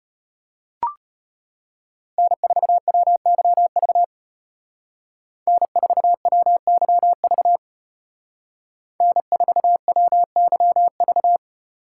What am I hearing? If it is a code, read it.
Morse code, heard as a keyed tone, about 30 wpm, 700 Hz. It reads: N4WYV N4WYV N4WYV
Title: Speed Racing - USA Call Signs 25wpm